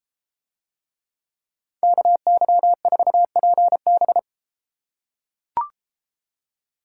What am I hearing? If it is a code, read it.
Morse code, heard as a keyed tone, about 31 wpm, 700 Hz. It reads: KY4PB